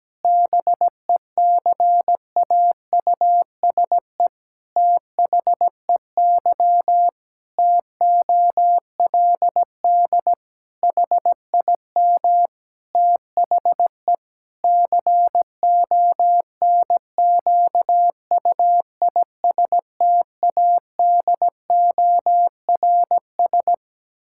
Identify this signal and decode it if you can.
Morse code, heard as a keyed tone, about 17 wpm, 700 Hz. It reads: BECAUSE THEY TOLD HIM THE CONQUISTADORS
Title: Pangram Challenge v2 - 17wpm